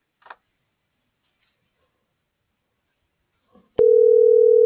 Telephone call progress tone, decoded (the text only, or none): ringback tone